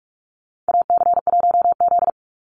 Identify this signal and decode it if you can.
Morse code, heard as a keyed tone, about 44 wpm, 700 Hz. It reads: AC1Z
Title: NA SOTA Activator Callsigns 45wpm